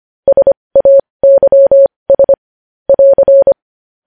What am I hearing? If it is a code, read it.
Morse code, heard as a keyed tone, about 25 wpm, spaced as for 20 wpm, 550 Hz. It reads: SAYS <AR>
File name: podcast / News headlines-1x-25x20